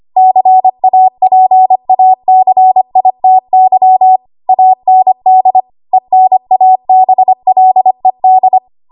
Morse code, heard as a keyed tone, about 25 wpm, 750 Hz. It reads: CAPACITY AND ENABLED
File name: Morse / 25WPM